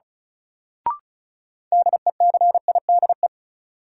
Morse code, heard as a keyed tone, about 36 wpm, 700 Hz. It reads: DECIDE
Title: Sentences from Top 500 Words - New Word Drill - 35wpm